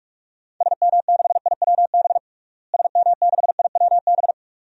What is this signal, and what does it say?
Morse code, heard as a keyed tone, about 42 wpm, 700 Hz. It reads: SM6IWB SM6IWB
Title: DXCC-100 Call Signs - 6 Characters - Repeated 2x - 45wpm